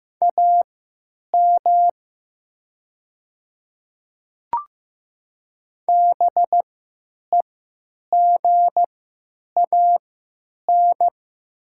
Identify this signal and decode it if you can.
Morse code, heard as a keyed tone, about 15 wpm, spaced as for 9 wpm, 700 Hz. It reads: AM BEGAN
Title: 3x Character Spacing - Top 500 Words - 15wpm